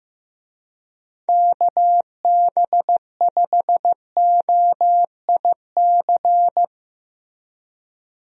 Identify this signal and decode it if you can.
Morse code, heard as a keyed tone, about 15 wpm, 700 Hz. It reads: KB5OIC
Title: USA Call Signs - 6 Characters - Repeated 2x - 15wpm